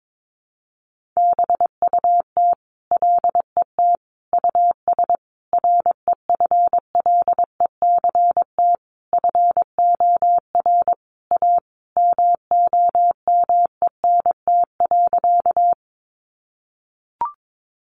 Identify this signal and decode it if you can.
Morse code, heard as a keyed tone, about 22 wpm, 700 Hz. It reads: BUT LET US REFLECT FOR A MOMENT.
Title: Sentences from Top 1500 Words 22wpm